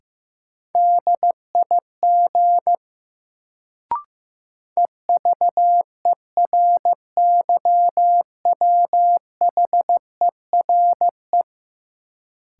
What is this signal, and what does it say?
Morse code, heard as a keyed tone, about 15 wpm, 700 Hz. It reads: DIG EVERYWHERE
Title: Sentences from Top 1600 Words - New Word Drill - 15wpm